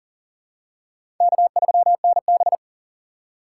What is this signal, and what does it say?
Morse code, heard as a keyed tone, about 40 wpm, 700 Hz. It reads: K3NB
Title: USA Call Signs - 4 Characters - 6x Character Spacing - 40wpm